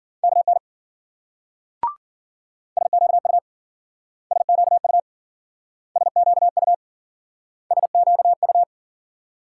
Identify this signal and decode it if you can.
Morse code, heard as a keyed tone, about 52 wpm, 700 Hz. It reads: KN SQU SQU SQU SQU